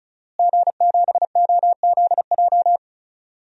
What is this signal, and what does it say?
Morse code, heard as a keyed tone, about 34 wpm, 700 Hz. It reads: G7OZJ
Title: DXCC-100 Call Signs - 5 Characters - Repeated 3x - 35wpm